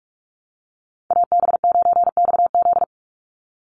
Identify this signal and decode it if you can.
Morse code, heard as a keyed tone, about 44 wpm, 700 Hz. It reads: AB9XZ
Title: USA Call Signs 45wpm